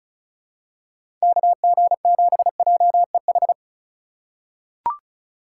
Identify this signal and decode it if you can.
Morse code, heard as a keyed tone, about 36 wpm, 700 Hz. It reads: KG7JEH
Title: USA Call Signs - 3x Character Spacing - 35wpm